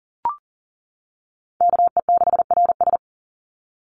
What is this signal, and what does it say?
Morse code, heard as a keyed tone, about 40 wpm, 700 Hz. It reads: KE6RS
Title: NA SOTA Chaser Callsigns 40wpm